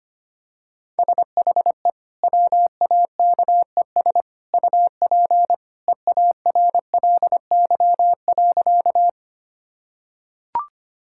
Morse code, heard as a keyed tone, about 25 wpm, 700 Hz. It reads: SHE WAKES UP EARLY.
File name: Easy Sentences with Hints - 25wpm